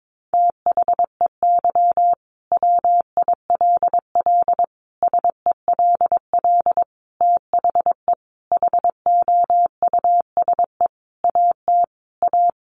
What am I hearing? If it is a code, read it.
Morse code, heard as a keyed tone, about 22 wpm, 700 Hz. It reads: THEY WILL SELL THE HOUSE AT A
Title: Sentences from Top 1200 Words 22wpm